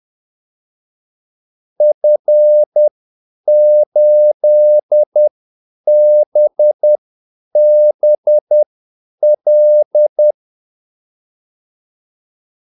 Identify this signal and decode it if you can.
Morse code, heard as a keyed tone, about 10 wpm, spaced as for 8 wpm, 600 Hz. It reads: F8BBL
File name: Callsigns-10-8wpm